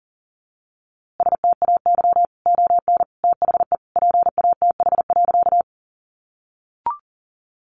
Morse code, heard as a keyed tone, about 40 wpm, 700 Hz. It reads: STAY ON THE PATH.
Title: A Sound Of Thunder 40wpm